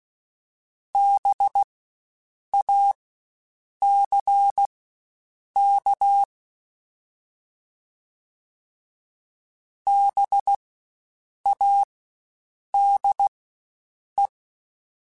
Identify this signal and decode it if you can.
Morse code, heard as a keyed tone, about 16 wpm, spaced as for 7 wpm, 800 Hz. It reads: BACK BADE
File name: Spaced Alphas / Spaced Alphas mp3s / Bs sp @15